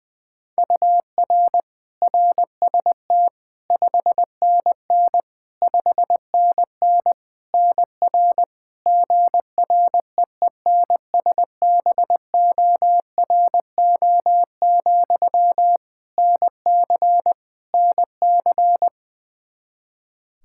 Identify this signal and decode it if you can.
Morse code, heard as a keyed tone, about 20 wpm, 700 Hz. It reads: UR RST 5NN 5NN NR GREENSBORO, NC NC
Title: Practice QSO - Exchange 1 - Beginning - 20wpm